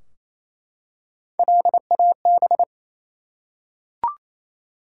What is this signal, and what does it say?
Morse code, heard as a keyed tone, about 28 wpm, 700 Hz. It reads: LAB
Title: Three Letter Words - 28wpm